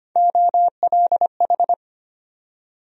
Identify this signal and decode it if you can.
Morse code, heard as a keyed tone, about 25 wpm, 700 Hz. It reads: OLH
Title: USA Call Sign Postfixes - 3 Letters 25wpm